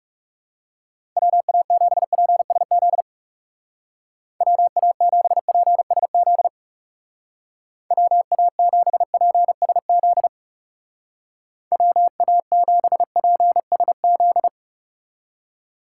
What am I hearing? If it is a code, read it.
Morse code, heard as a keyed tone, about 38 wpm, 700 Hz. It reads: WA7PSZ WA7PSZ WA7PSZ WA7PSZ
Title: Speed Racing - USA 2x3 Call Signs 30wpm